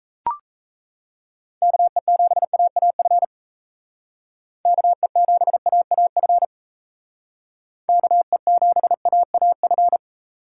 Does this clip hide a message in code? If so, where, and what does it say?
Yes: Morse code from 1.6 to 10.0 s, heard as a keyed tone, about 36 wpm, 700 Hz. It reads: KE7AAF KE7AAF KE7AAF